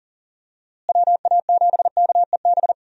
Morse code, heard as a keyed tone, about 40 wpm, 700 Hz. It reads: WA7KEB
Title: USA Call Signs - 6 Characters - Repeated 2x - 40wpm